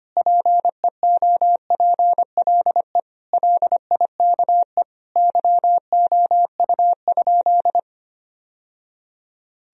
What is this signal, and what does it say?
Morse code, heard as a keyed tone, about 25 wpm, 700 Hz. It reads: PEOPLE LIKE YOU?
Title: Sentences from Top 1000 Words - Review Part 1 - 25wpm